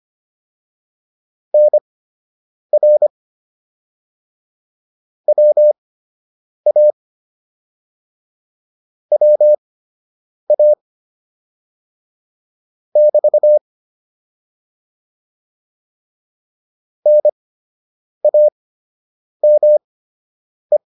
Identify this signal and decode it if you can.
Morse code, heard as a keyed tone, about 25 wpm, spaced as for 8 wpm, 600 Hz. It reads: NR WA WA <BT> NAME